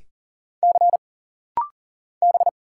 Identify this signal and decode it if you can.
Morse code, heard as a keyed tone, about 39 wpm, 700 Hz. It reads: C B
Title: Single Letter-Number Alt - Rapid-Fire 40wpm